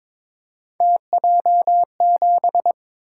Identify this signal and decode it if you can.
Morse code, heard as a keyed tone, about 23 wpm, 700 Hz. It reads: TJ7
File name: practice / 22wpm / Three Letter-Number Combinations - 22wpm